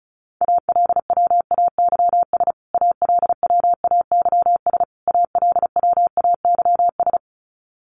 Morse code, heard as a keyed tone, about 34 wpm, 700 Hz. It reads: ALWAYS ALWAYS ALWAYS
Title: Top 300 Words Repeated 35wpm